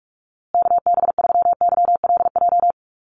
Morse code, heard as a keyed tone, about 44 wpm, 700 Hz. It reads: KB3YLJ
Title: USA Call Signs Repeated 3x 45wpm